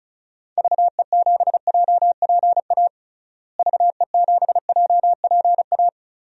Morse code, heard as a keyed tone, about 36 wpm, 700 Hz. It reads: VE7JPA VE7JPA
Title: DXCC-100 Call Signs - 6 Characters - Repeated 2x - 35wpm